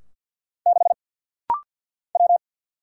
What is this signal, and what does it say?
Morse code, heard as a keyed tone, about 50 wpm, 700 Hz. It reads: 6 W
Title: Single Letter-Number Alt - Rapid-Fire 50wpm